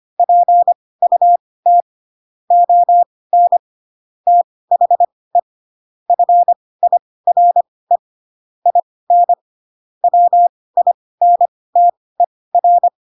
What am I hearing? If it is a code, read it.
Morse code, heard as a keyed tone, about 25 wpm, spaced as for 18 wpm, 700 Hz. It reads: PUT ON THE FIRE IN WINTER
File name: Pinocchio / Pinocchio ch1 12 25 ik1hge